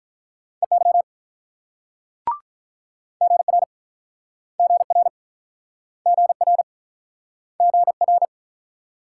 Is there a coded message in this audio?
Yes: Morse code from 0.6 to 8.3 s, heard as a keyed tone, about 50 wpm, 700 Hz. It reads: EY GR GR GR GR